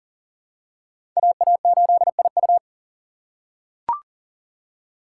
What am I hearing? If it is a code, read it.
Morse code, heard as a keyed tone, about 40 wpm, 700 Hz. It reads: AA8IU
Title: USA Call Signs - 5 Characters - 2x Character Spacing - 40wpm